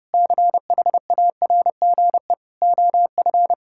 Morse code, heard as a keyed tone, about 30 wpm, 700 Hz. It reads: CHARGE OF
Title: Sentences from Top 1000 Words - Review Part 6 - 30wpm